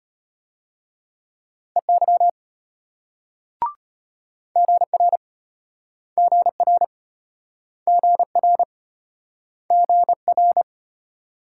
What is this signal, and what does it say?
Morse code, heard as a keyed tone, about 34 wpm, 700 Hz. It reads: EY GR GR GR GR